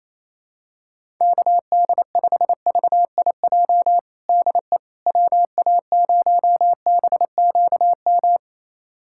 Morse code, heard as a keyed tone, about 28 wpm, 700 Hz. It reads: KD5VIJ DE WA0BQM
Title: USA Callsign DE Callsign - 6 Characters - 28wpm